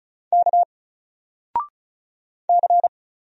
Morse code, heard as a keyed tone, about 35 wpm, 700 Hz. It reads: K C
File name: Single-Letter-Number-35wpm